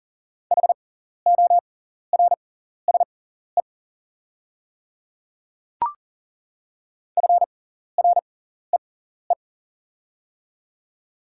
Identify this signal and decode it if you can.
Morse code, heard as a keyed tone, about 40 wpm, spaced as for 14 wpm, 690 Hz. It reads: HORSE FREE
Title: 6x Character Spacing - Top 500 Words - 40wpm